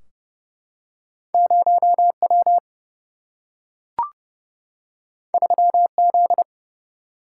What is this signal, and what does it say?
Morse code, heard as a keyed tone, about 30 wpm, 700 Hz. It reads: 0W 3Z
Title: Single-Letter Single-Number 30wpm